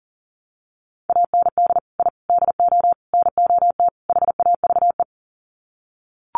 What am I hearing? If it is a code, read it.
Morse code, heard as a keyed tone, about 40 wpm, 700 Hz. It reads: AND I DO NOT HAVE
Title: Sets of 5 Words Encore 40wpm